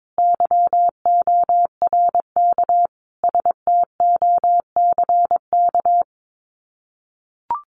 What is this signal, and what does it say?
Morse code, heard as a keyed tone, about 22 wpm, 700 Hz. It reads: YORK STOCK
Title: Sets of 5 Words Encore 22wpm